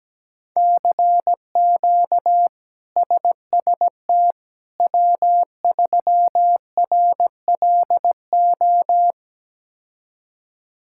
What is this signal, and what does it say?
Morse code, heard as a keyed tone, about 17 wpm, 700 Hz. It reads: CQ SST W3RLO
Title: K1USN SST QSO Practice - 17wpm